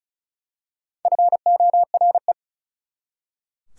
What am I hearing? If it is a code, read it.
Morse code, heard as a keyed tone, about 36 wpm, 700 Hz. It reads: FORE
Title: Prefixes and Suffixes 35wpm